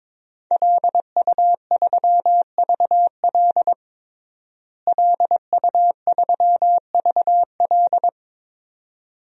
Morse code, heard as a keyed tone, about 22 wpm, 700 Hz. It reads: LU3VL LU3VL
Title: DXCC-100 Call Signs - 5 Characters - Repeated 2x - 22wpm